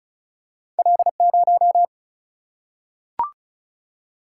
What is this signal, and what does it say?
Morse code, heard as a keyed tone, about 36 wpm, 700 Hz. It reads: L0
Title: Single-Letter Single-Number 35wpm